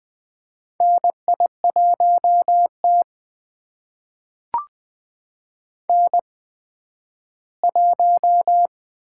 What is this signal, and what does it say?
Morse code, heard as a keyed tone, about 21 wpm, spaced as for 12 wpm, 700 Hz. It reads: NI1T N1
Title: USA Call Signs - 8x Character Spacing - 20wpm